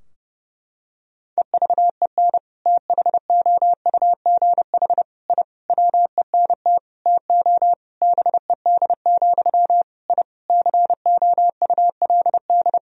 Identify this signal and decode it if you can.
Morse code, heard as a keyed tone, about 31 wpm, 700 Hz. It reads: EVEN THOUGH I WENT TO BED, I COULD
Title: Sentences from Top 1000 Words - Review Part 2 - 30wpm